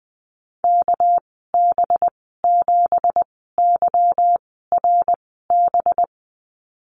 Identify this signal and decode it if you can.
Morse code, heard as a keyed tone, about 20 wpm, spaced as for 14 wpm, 700 Hz. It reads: KB7YRB
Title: 238 Lesson 38g - Letter q - Call Signs 20wpm